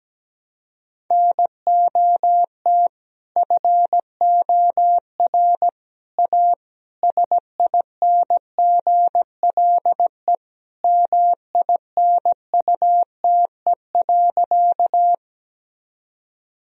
Morse code, heard as a keyed tone, about 17 wpm, 700 Hz. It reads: NOT FOR A SINGLE MINUTE.
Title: Sentences from Top 1000 Words - Review Part 4 - 17wpm